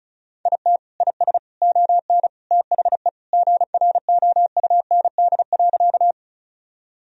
Morse code, heard as a keyed tone, about 34 wpm, 700 Hz. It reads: IT IS ON THE GROUND.
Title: Sentences from Top 1000 Words - Review Part 2 - 35wpm